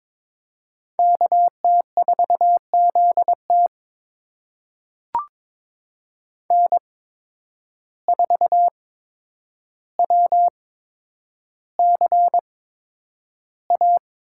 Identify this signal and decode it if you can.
Morse code, heard as a keyed tone, about 23 wpm, spaced as for 10 wpm, 700 Hz. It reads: KT4ZT N4WCA
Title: USA Call Signs - 5 Characters - 8x Character Spacing - 22wpm